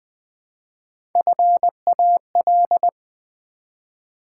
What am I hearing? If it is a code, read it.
Morse code, heard as a keyed tone, about 20 wpm, 700 Hz. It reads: FAL
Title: Build A Word - 1 Letter At A Time - 5-Letter Words - 20wpm